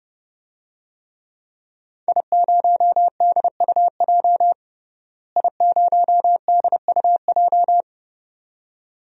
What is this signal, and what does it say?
Morse code, heard as a keyed tone, about 30 wpm, 700 Hz. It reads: I0DUJ I0DUJ
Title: DXCC-100 Call Signs - 5 Characters - Repeated 2x - 30wpm